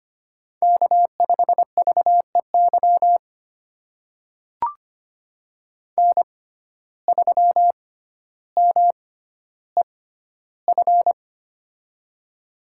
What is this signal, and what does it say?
Morse code, heard as a keyed tone, about 25 wpm, spaced as for 13 wpm, 700 Hz. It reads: K5VEY N3MEF
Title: USA Call Signs - 5 Characters - 6x Character Spacing - 25wpm